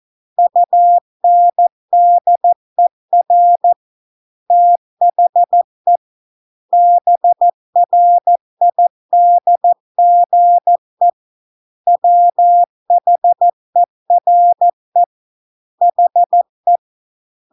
Morse code, heard as a keyed tone, about 14 wpm, 700 Hz. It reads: UNDER THE BRIDGE WHERE HE